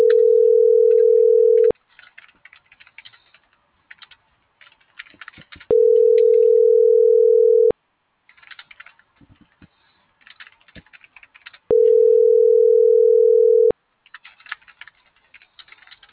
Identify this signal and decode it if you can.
Ringback tone (North America), 440 + 480 Hz.